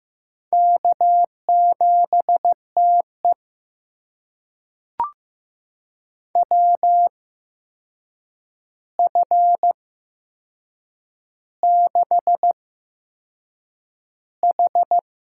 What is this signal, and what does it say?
Morse code, heard as a keyed tone, about 15 wpm, spaced as for 6 wpm, 700 Hz. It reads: K7TE WF6H
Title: USA Call Signs - 4 Characters - 8x Character Spacing - 15wpm